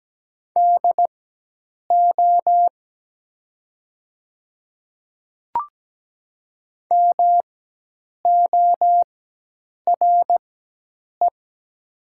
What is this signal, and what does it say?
Morse code, heard as a keyed tone, about 17 wpm, spaced as for 8 wpm, 700 Hz. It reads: DO MORE
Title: 4x Character Spacing - Top 100 Words - 17wpm